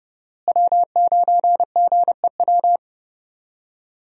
Morse code, heard as a keyed tone, about 30 wpm, 700 Hz. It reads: W9GEW
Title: USA Call Signs - 5 Characters - Repeated 2x - 30wpm